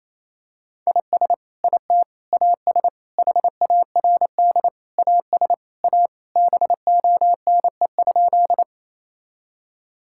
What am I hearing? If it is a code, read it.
Morse code, heard as a keyed tone, about 28 wpm, 700 Hz. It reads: IS IT AS HARD AS A BONE?
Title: Sentences from Top 1600 Words 28wpm